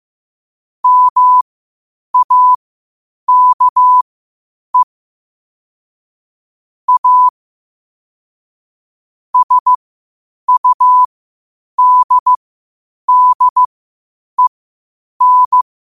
Morse code, heard as a keyed tone, about 15 wpm, spaced as for 8 wpm, 1000 Hz. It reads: MAKE A SUDDEN
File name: qotdfiles / 8wpm